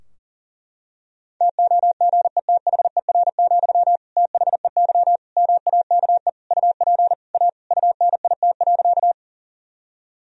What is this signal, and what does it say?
Morse code, heard as a keyed tone, about 40 wpm, 700 Hz. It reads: TOGETHER, THEY MAKE UP A UNIT.
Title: Sentences from Top 1000 Words - Review Part 2 - 40wpm